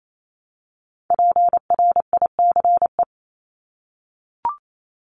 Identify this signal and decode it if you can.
Morse code, heard as a keyed tone, about 28 wpm, 700 Hz. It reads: PRICE
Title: Top 501-750 Words 28wpm